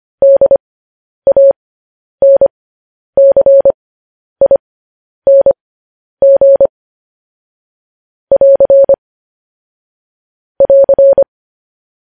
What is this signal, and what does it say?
Morse code, heard as a keyed tone, about 25 wpm, spaced as for 10 wpm, 550 Hz. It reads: DANCING <AR> <AR>